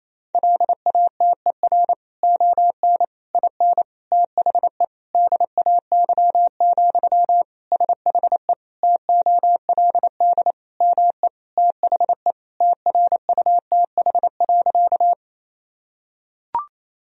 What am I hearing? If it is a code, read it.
Morse code, heard as a keyed tone, about 28 wpm, 700 Hz. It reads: LATER ON IN THE DAY, SHE TOLD ME THE TRUTH.
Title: Sentences from Top 1000 Words - Review Part 4 - 28wpm